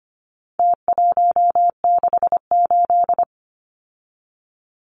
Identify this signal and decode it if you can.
Morse code, heard as a keyed tone, about 25 wpm, 700 Hz. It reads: T168
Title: Numbers - Contest Style - 4-Digit 25wpm